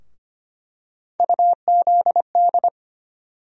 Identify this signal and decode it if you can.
Morse code, heard as a keyed tone, about 25 wpm, 700 Hz. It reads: UZD
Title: USA Call Sign Postfixes - 3 Letters 25wpm